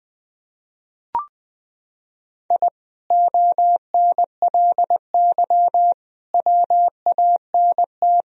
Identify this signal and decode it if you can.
Morse code, heard as a keyed tone, about 20 wpm, 700 Hz. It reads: I ONLY WANT
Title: Sentences from Top 1000 Words - Review Part 1 - 20wpm